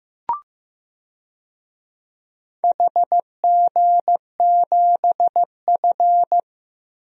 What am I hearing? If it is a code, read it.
Morse code, heard as a keyed tone, about 15 wpm, 700 Hz. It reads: HG7F
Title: DXCC-100 Call Signs - 4 Characters - 15wpm